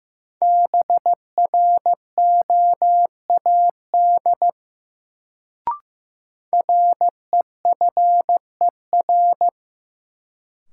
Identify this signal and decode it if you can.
Morse code, heard as a keyed tone, about 15 wpm, 700 Hz. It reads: BROAD REFER
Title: Sentences from Top 1000 Words - New Word Drill - 15wpm